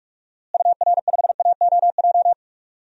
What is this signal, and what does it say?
Morse code, heard as a keyed tone, about 43 wpm, 700 Hz. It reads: UR5AOJ
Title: DXCC-100 Call Signs - 6 Characters - Repeated 2x - 45wpm